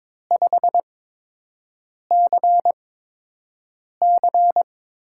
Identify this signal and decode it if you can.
Morse code, heard as a keyed tone, about 23 wpm, spaced as for 6 wpm, 700 Hz. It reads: 5CC
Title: USA Call Signs - 4 Characters - 8x Character Spacing - 22wpm